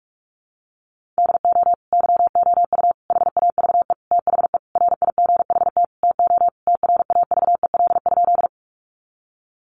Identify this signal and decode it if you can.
Morse code, heard as a keyed tone, about 44 wpm, 700 Hz. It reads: DO YOU HAVE THE RIGHT TO TRAVEL?
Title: Sentences from Top 1400 Words 45wpm